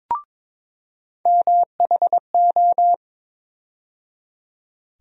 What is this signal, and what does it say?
Morse code, heard as a keyed tone, about 22 wpm, 700 Hz. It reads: MHO
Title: USA Call Sign Postfixes - 3 Letters 22wpm